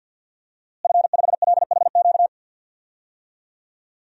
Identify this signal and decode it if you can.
Morse code, heard as a keyed tone, about 52 wpm, 700 Hz. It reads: F5LHQ
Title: DXCC-100 Call Signs - 5 Characters - Repeated 3x - 50wpm